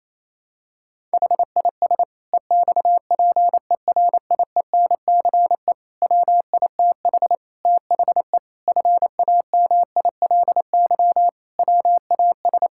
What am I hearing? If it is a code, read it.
Morse code, heard as a keyed tone, about 28 wpm, 700 Hz. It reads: HIS EXPERIENCE WITH THE FAMILY WAS
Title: Sentences from Top 1000 Words - Review Part 6 - 28wpm